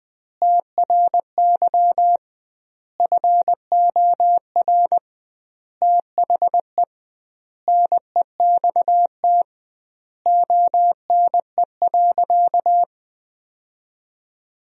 Morse code, heard as a keyed tone, about 20 wpm, 700 Hz. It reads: TRY FOR THE NEXT ONE.
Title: 2x Word-Spacing - Sentences from Top 400 Words - 20wpm